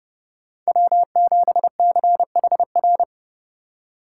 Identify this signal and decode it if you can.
Morse code, heard as a keyed tone, about 36 wpm, 700 Hz. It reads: W7CHR
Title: Speed Racing - USA 1x3 Call Signs 30wpm